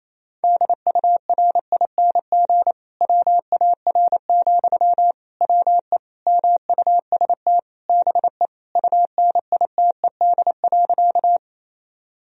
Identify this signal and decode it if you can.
Morse code, heard as a keyed tone, about 28 wpm, 700 Hz. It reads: DURING WAR, WE MUST BE UNITED.
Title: Sentences from Top 1000 Words - Review Part 4 - 28wpm